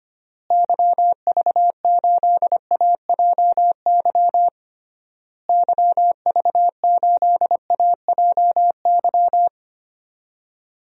Morse code, heard as a keyed tone, about 25 wpm, 700 Hz. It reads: YV8AJY YV8AJY
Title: DXCC-100 Call Signs - 6 Characters - Repeated 3x - 25wpm